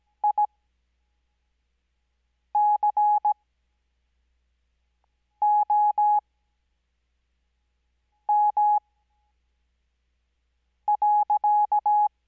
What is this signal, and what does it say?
Morse code, heard as a keyed tone, about 17 wpm, spaced as for 4 wpm, 840 Hz. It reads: ICOM.
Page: Farnsworth Spacing: 15WPM Character speed Spaced at 5WPM.